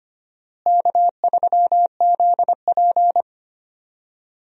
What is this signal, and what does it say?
Morse code, heard as a keyed tone, about 25 wpm, 700 Hz. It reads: K3ZP
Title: USA Call Signs - 4 Characters - Repeated 3x - 25wpm